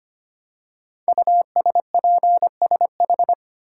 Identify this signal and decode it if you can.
Morse code, heard as a keyed tone, about 25 wpm, 700 Hz. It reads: USPSH
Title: Code Groups - 5 Letters - 25wpm